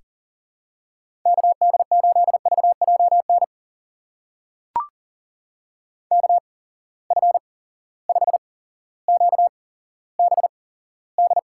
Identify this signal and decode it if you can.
Morse code, heard as a keyed tone, about 41 wpm, spaced as for 17 wpm, 700 Hz. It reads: KD8VJN KF5QBD
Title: USA Call Signs - 6 Characters - 8x Character Spacing - 40wpm